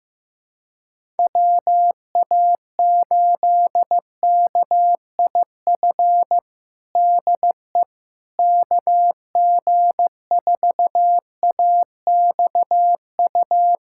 Morse code, heard as a keyed tone, about 15 wpm, 700 Hz. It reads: WA8KIF DE KG4AXU
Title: USA Callsign DE Callsign - 6 Characters - 15wpm